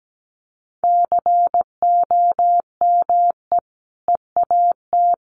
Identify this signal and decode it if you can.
Morse code, heard as a keyed tone, about 17 wpm, 700 Hz. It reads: COME EAT
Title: Sentences from Top 300 Words 17wpm